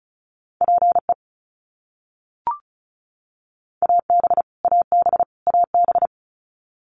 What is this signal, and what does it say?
Morse code, heard as a keyed tone, about 36 wpm, 700 Hz. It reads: PE AB AB AB
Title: Canadian Provinces and Territories - Repeated 3x - 35wpm